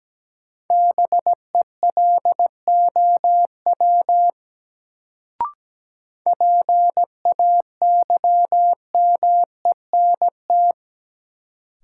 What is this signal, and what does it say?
Morse code, heard as a keyed tone, about 17 wpm, 700 Hz. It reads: BELOW PAYMENT
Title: Sentences from Top 1200 Words - New Word Drill Alt - 17wpm